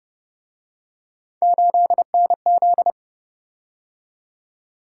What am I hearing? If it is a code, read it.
Morse code, heard as a keyed tone, about 30 wpm, 700 Hz. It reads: 8NZ
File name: Speed-Racing - Three Letter-Number Combinations - 25wpm